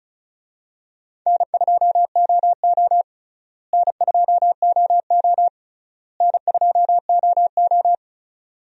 Morse code, heard as a keyed tone, about 34 wpm, 700 Hz. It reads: N2OO N2OO N2OO
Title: USA Call Signs - 4 Characters - Repeated 3x - 35wpm